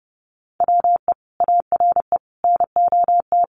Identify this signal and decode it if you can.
Morse code, heard as a keyed tone, about 30 wpm, 700 Hz. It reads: WE ARE NOT
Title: Sentences from Top 1000 Words 30wpm